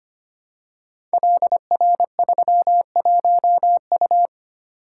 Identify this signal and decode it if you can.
Morse code, heard as a keyed tone, about 25 wpm, 700 Hz. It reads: LR31U
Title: Code Groups - 5 Letter:Numbers - 25wpm